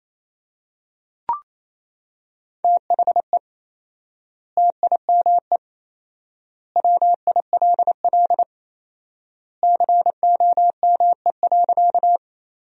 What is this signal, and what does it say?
Morse code, heard as a keyed tone, about 28 wpm, 700 Hz. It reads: THE TIME WILL COME.
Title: 4x Word-Spacing - Sentences from Top 100 Words - 28wpm